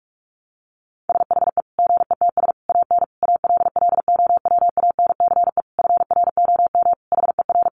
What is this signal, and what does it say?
Morse code, heard as a keyed tone, about 44 wpm, 700 Hz. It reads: SHE GETS AN ALLOWANCE FROM HER
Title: Easy Phrases 45wpm